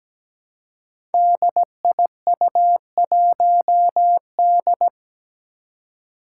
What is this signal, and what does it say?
Morse code, heard as a keyed tone, about 17 wpm, 700 Hz. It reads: DIU1D
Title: Five Challenging Letter-Number Combinations 17wpm